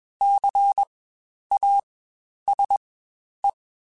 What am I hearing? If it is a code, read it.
Morse code, heard as a keyed tone, about 21 wpm, spaced as for 10 wpm, 800 Hz. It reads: CASE
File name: Spaced Alphas / Spaced Alphas mp3s / Cs sp @20